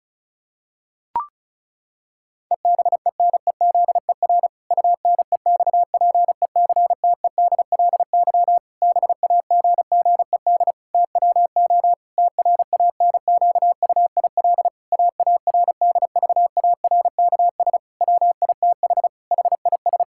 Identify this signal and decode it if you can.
Morse code, heard as a keyed tone, about 34 wpm, 700 Hz. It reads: EBENEZER UNEXPECTEDLY BAGGED TWO TRANQUIL AARDVARKS WITH HIS
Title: Pangram Challenge v2 - 35wpm